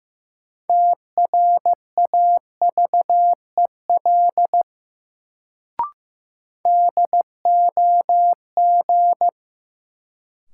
Morse code, heard as a keyed tone, about 15 wpm, 700 Hz. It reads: TRAVEL DOG
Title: Sentences from Top 400 Words - New Word Drill - 15wpm